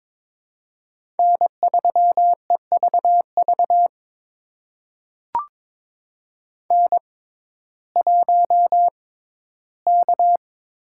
Morse code, heard as a keyed tone, about 22 wpm, spaced as for 14 wpm, 700 Hz. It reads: N3EVV N1K
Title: USA Call Signs - 5 Characters - 6x Character Spacing - 22wpm